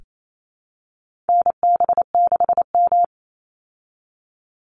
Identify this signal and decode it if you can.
Morse code, heard as a keyed tone, about 28 wpm, 700 Hz. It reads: NB6M
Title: Call Signs Repeated 3x - 28wpm